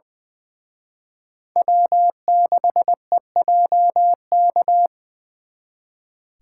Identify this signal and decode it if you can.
Morse code, heard as a keyed tone, about 24 wpm, 700 Hz. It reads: W6EJK
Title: Speed Racing - USA 1x3 Call Signs 20wpm